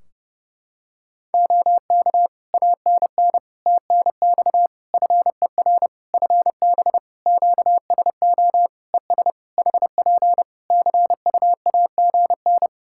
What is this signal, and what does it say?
Morse code, heard as a keyed tone, about 32 wpm, 700 Hz. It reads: OK ANN TNX FER FB QSO ES HP CUAGN
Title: K1USN SST QSO Practice 30wpm